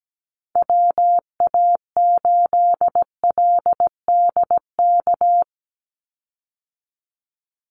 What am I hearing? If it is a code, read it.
Morse code, heard as a keyed tone, about 17 wpm, 700 Hz. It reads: WA8LDK
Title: USA Call Signs 17wpm